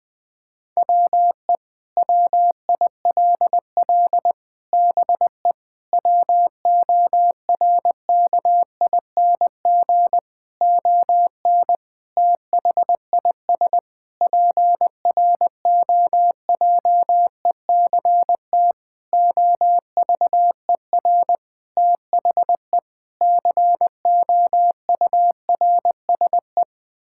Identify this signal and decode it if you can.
Morse code, heard as a keyed tone, about 20 wpm, 700 Hz. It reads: WE WILL BE WORKING ON THIS PROJECT OVER THE COURSE
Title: Sentences with 6-Word Phrases - Spoken Beginning - 20wpm